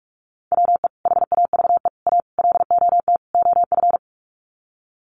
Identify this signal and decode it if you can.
Morse code, heard as a keyed tone, about 44 wpm, 700 Hz. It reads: WE HAVE A LOT OF
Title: Sets of 5 Words 45wpm